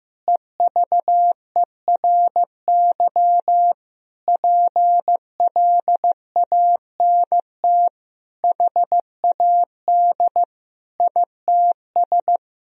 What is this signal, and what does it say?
Morse code, heard as a keyed tone, about 16 wpm, 700 Hz. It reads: EVERY PLANT HAD ITS
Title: Sentences from Top 1000 Words - Review Part 1 - 15wpm